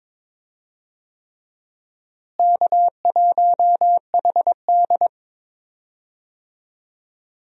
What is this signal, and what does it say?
Morse code, heard as a keyed tone, about 22 wpm, 700 Hz. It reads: K1HD
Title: USA Call Signs - 4 Characters - Repeated 3x - 22wpm